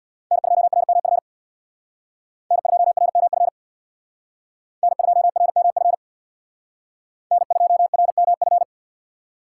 Morse code, heard as a keyed tone, about 65 wpm, 700 Hz. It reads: N2RMF N2RMF N2RMF N2RMF